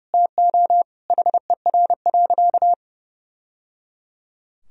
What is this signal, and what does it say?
Morse code, heard as a keyed tone, about 30 wpm, 700 Hz. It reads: TO HER.
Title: Sentences from Top 1000 Words - Review Part 1 - 30wpm